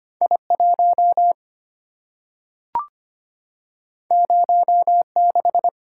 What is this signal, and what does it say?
Morse code, heard as a keyed tone, about 25 wpm, 700 Hz. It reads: I1 06
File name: Two Letter-Number Combinations - 25wpm